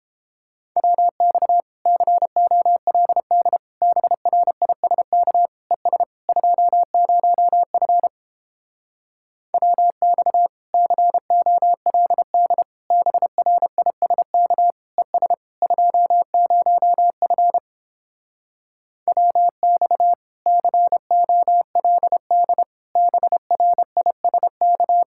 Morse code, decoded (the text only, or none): WX COLD BRISK ES 20F WX COLD BRISK ES 20F WX COLD BRISK